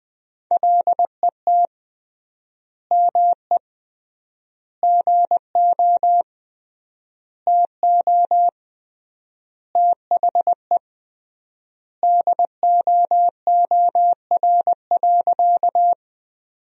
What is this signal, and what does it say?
Morse code, heard as a keyed tone, about 20 wpm, 700 Hz. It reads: LET ME GO TO THE DOOR.
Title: 3x Word-Spacing - Sentences from Top 300 Words - 20wpm